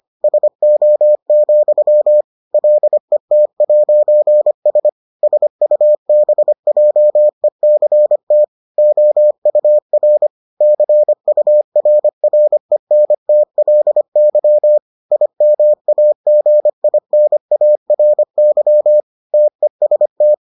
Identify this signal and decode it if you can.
Morse code, heard as a keyed tone, about 25 wpm, 600 Hz. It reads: SO, LET'S SUBJECT OUR CURRENTLY IMAGINARY TEST